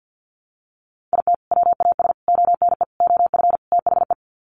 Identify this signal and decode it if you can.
Morse code, heard as a keyed tone, about 48 wpm, 700 Hz. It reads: IT WAS ONE OF THE
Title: Sets of 5 Words 50wpm